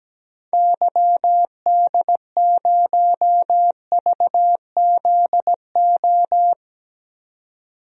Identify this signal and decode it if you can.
Morse code, heard as a keyed tone, about 17 wpm, 700 Hz. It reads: YD0VZO
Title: DXCC-100 Call Signs - 6 Characters - 17wpm